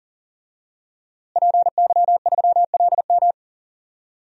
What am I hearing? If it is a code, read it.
Morse code, heard as a keyed tone, about 40 wpm, 700 Hz. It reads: PY3LM
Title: DXCC-100 Call Signs - 5 Characters - Repeated 3x - 40wpm